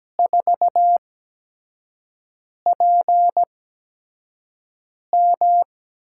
Morse code, heard as a keyed tone, about 17 wpm, spaced as for 5 wpm, 700 Hz. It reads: 4PM
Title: USA Call Signs - 8x Character Spacing - 17wpm